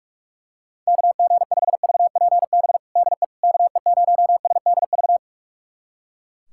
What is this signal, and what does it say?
Morse code, heard as a keyed tone, about 41 wpm, 700 Hz. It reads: KG5VPB DE KE0SDV
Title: USA Callsign DE Callsign 45wpm